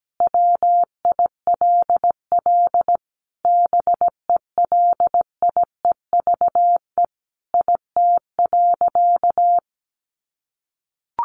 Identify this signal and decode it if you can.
Morse code, heard as a keyed tone, about 17 wpm, 700 Hz. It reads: WILL BELIEVE IT.
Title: Sentences from Top 1000 Words 17wpm